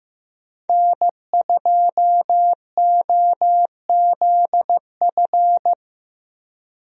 Morse code, heard as a keyed tone, about 15 wpm, 700 Hz. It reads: N2OZF
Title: USA Call Signs - 5 Characters - Repeated 2x - 15wpm